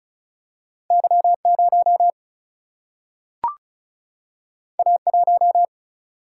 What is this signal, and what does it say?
Morse code, heard as a keyed tone, about 36 wpm, 700 Hz. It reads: Y 0 A 1
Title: Single-Letter Single-Number 35wpm